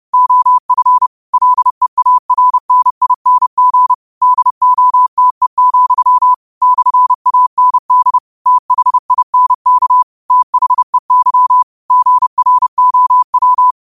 Morse code, heard as a keyed tone, about 30 wpm, 1000 Hz. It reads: OF LEARNING DOTE, /AND THINK THEY GROW
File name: 30wpm